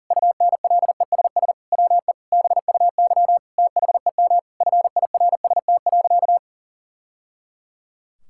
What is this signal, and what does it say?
Morse code, heard as a keyed tone, about 40 wpm, 700 Hz. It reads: UNLESS WE BUY THEM FIRST.
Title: Sentences from Top 1000 Words - Review Part 6 - 40wpm